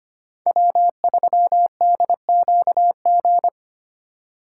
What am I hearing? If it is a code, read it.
Morse code, heard as a keyed tone, about 29 wpm, 700 Hz. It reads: W3DQG
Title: Speed Racing - USA 1x3 Call Signs 25wpm